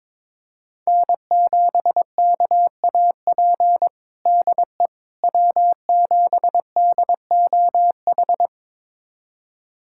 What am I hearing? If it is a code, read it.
Morse code, heard as a keyed tone, about 22 wpm, 700 Hz. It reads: N7KAP DE W7DOH
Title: USA Callsign DE Callsign - 5 Characters - 22wpm